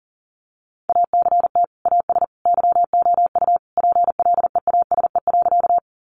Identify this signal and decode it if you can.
Morse code, heard as a keyed tone, about 40 wpm, 700 Hz. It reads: ACT AS YOU PLEASE.
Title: Sentences from Top 1500 Words 40wpm